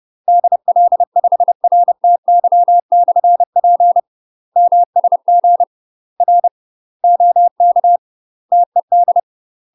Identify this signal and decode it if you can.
Morse code, heard as a keyed tone, about 30 wpm, 700 Hz. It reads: DL5RTY/P MSG R OK TED
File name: UK MASTER QSOs TRAINING / 30WPM / UK MASTER QSOs 0029